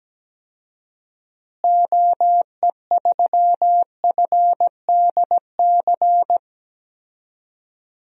Morse code, heard as a keyed tone, about 17 wpm, 700 Hz. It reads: OE3FDC
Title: DXCC-100 Call Signs - 6 Characters - Repeated 2x - 17wpm